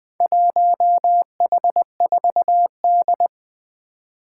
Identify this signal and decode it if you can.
Morse code, heard as a keyed tone, about 20 wpm, 700 Hz. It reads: 1H4D
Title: Four Challenging Letter-Number Combinations 20wpm